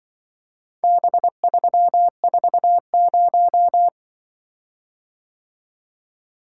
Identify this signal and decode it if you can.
Morse code, heard as a keyed tone, about 24 wpm, 700 Hz. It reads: B340
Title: Speed-Racing - Four Letter-Number Combinations - 20wpm